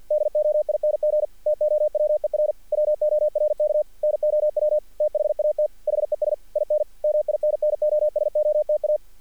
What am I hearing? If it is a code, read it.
Morse code, heard as a keyed tone, about 48 wpm, 600 Hz. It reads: DOING TOWER WORK NOW THAT HES IN MINNOSOTA
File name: archived / CRSnet